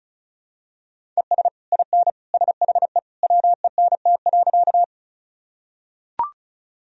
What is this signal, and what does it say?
Morse code, heard as a keyed tone, about 36 wpm, 690 Hz. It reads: ES IN SHE WENT.
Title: Sentences from Top 200 Words and Abbreviations 35wpm